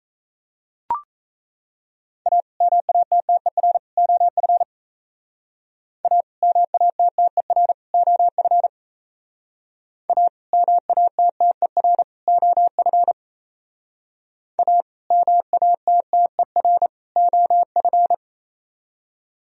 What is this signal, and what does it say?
Morse code, heard as a keyed tone, about 37 wpm, 700 Hz. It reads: A MATTER OF A MATTER OF A MATTER OF A MATTER OF